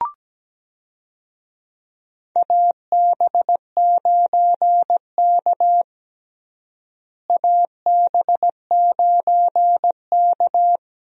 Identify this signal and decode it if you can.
Morse code, heard as a keyed tone, about 17 wpm, 700 Hz. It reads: AB9K AB9K
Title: USA Call Signs - 4 Characters - Repeated 2x - 17wpm